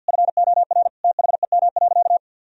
Morse code, heard as a keyed tone, about 52 wpm, 700 Hz. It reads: FOR THEM.
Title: Sentences from Top 1000 Words - Review Part 1 - 50wpm